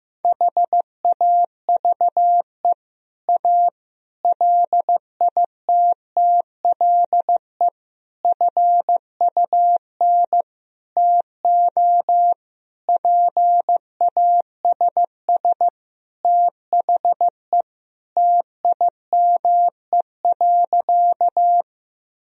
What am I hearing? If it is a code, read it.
Morse code, heard as a keyed tone, about 16 wpm, 700 Hz. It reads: HAVE A LITTLE FUN TO PASS THE TIME.
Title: Sentences from Top 1000 Words - Review Part 4 - 15wpm